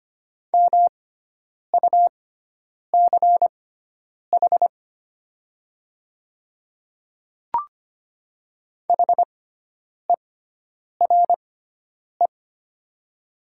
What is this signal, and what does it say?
Morse code, heard as a keyed tone, about 25 wpm, spaced as for 9 wpm, 700 Hz. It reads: MUCH HERE
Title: 6x Character Spacing - Top 300 Words - 25wpm